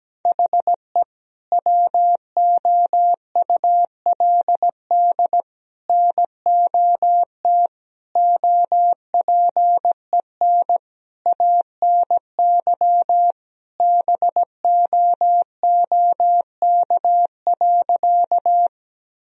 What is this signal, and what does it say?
Morse code, heard as a keyed tone, about 17 wpm, 700 Hz. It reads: HE WOULD NOT OPEN ANY BOOK.
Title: Sentences from Top 1000 Words - Review Part 1 - 17wpm